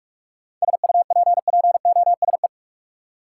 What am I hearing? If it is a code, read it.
Morse code, heard as a keyed tone, about 46 wpm, 700 Hz. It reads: SUPPOSE
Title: Seven Letter Words - 45wpm